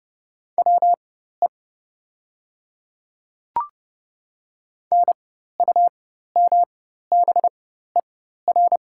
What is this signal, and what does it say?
Morse code, heard as a keyed tone, about 30 wpm, spaced as for 14 wpm, 690 Hz. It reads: WE NUMBER
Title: 4x Character Spacing - Top 200 Words - 30wpm